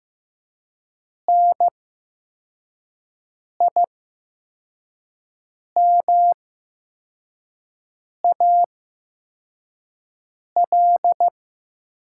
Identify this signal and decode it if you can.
Morse code, heard as a keyed tone, about 15 wpm, spaced as for 4 wpm, 700 Hz. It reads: NIMAL
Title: 8x Character Spacing - Top 200 Words - 15wpm